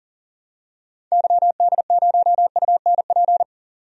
Morse code, heard as a keyed tone, about 40 wpm, 700 Hz. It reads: YD0UNP
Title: DXCC-100 Call Signs - 6 Characters - 40wpm